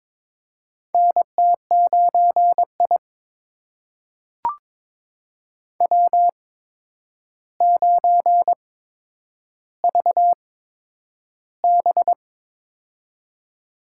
Morse code, heard as a keyed tone, about 22 wpm, spaced as for 9 wpm, 700 Hz. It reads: NT9I W9VB
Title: USA Call Signs - 4 Characters - 8x Character Spacing - 22wpm